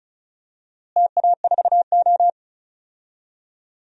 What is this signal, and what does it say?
Morse code, heard as a keyed tone, about 36 wpm, 700 Hz. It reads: TA4O
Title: DXCC-100 Call Signs - 4 Characters - Repeated 3x - 35wpm